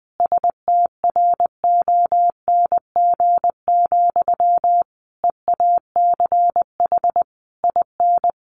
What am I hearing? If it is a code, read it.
Morse code, heard as a keyed tone, about 20 wpm, 700 Hz. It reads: STRONG, EACH IN
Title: Sentences from Top 800 Words 20wpm